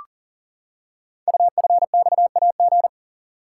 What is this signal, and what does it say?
Morse code, heard as a keyed tone, about 43 wpm, 700 Hz. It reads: UFXAG
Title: Code Groups - 5 Letters - 40wpm